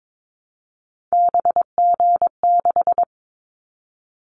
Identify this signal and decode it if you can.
Morse code, heard as a keyed tone, about 22 wpm, 700 Hz. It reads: BG6
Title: Callsign Prefixes for Top 100 DXCC - 22wpm